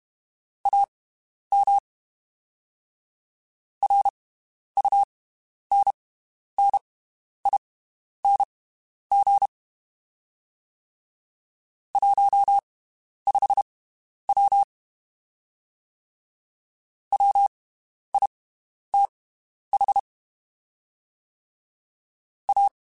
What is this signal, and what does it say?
Morse code, heard as a keyed tone, about 32 wpm, spaced as for 11 wpm, 800 Hz. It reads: AM RUNNING 15W WITH A
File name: (spaced) / QSOs Spaced-mp3s / WD0BGP 1 @30 sp